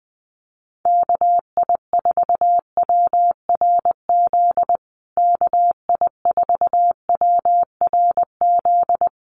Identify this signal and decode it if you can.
Morse code, heard as a keyed tone, about 20 wpm, 700 Hz. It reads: KI4WRZ KI4WRZ
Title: USA Call Signs Repeated 3x 20wpm